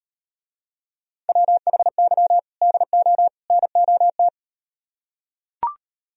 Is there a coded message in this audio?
Yes: Morse code from 1.3 to 4.3 s, heard as a keyed tone, about 38 wpm, 700 Hz. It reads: WHY DO NOT